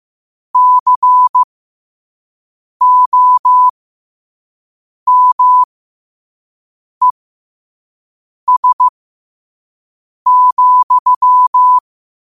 Morse code, heard as a keyed tone, about 15 wpm, spaced as for 5 wpm, 1000 Hz. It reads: COMES,
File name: qotdfiles / 5wpm